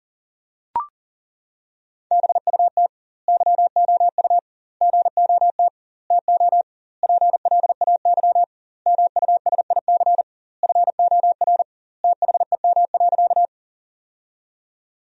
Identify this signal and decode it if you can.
Morse code, heard as a keyed tone, about 40 wpm, 700 Hz. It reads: BUT YOU GOT TO PLAY MUSIC FOR THEM.
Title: 2x Word-Spacing - Sentences from Top 300 Words - 40wpm